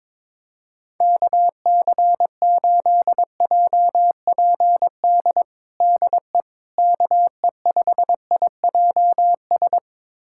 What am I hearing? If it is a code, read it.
Morse code, heard as a keyed tone, about 22 wpm, 700 Hz. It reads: KC8JPD DE KE5IJS
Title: USA Callsign DE Callsign - 6 Characters - 22wpm